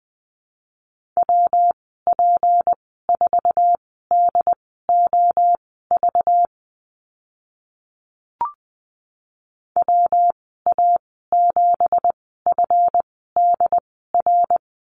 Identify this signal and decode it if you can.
Morse code, heard as a keyed tone, about 20 wpm, spaced as for 14 wpm, 700 Hz. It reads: WP4DOV WA7FDR
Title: 193 Lesson 31g - Letter v - Call Signs 20wpm